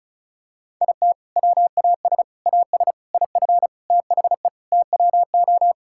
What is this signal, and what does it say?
Morse code, heard as a keyed tone, about 36 wpm, 700 Hz. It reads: IT WAS AS IF THE TWO
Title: Sentences from Top 1000 Words - Review Part 6 - 35wpm